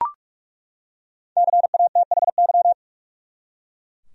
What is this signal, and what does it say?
Morse code, heard as a keyed tone, about 41 wpm, 700 Hz. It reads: CATHY
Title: USA Common Names - 45wpm